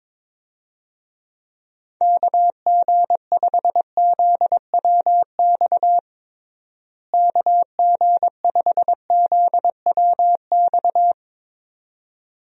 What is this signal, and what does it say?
Morse code, heard as a keyed tone, about 23 wpm, 700 Hz. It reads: KG5ZWX KG5ZWX
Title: USA Call Signs - 6 Characters - Repeated 2x - 22wpm